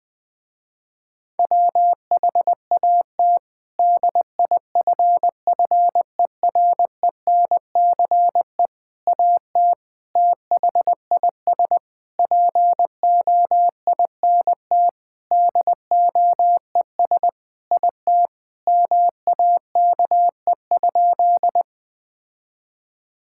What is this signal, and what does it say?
Morse code, heard as a keyed tone, about 20 wpm, 700 Hz. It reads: WHAT DIFFERENCE AT THIS POINT DOES IT MAKE?
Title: Sentences with 8-Word Phrases - Spoken Beginning - 20wpm